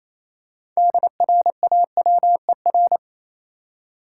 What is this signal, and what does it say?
Morse code, heard as a keyed tone, about 28 wpm, 700 Hz. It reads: DRAWER
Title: Six Letter Words - 28wpm